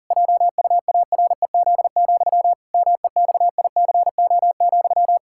Morse code, heard as a keyed tone, about 40 wpm, 700 Hz. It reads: JUAREZ, MEXICO,
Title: Pangram Challenge v2 - 40wpm